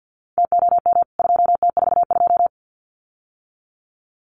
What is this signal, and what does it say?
Morse code, heard as a keyed tone, about 50 wpm, 700 Hz. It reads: TOM 2T42
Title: CWT Exchange 50wpm